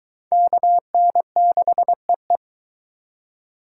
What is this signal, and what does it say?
Morse code, heard as a keyed tone, about 22 wpm, 700 Hz. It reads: KN6EE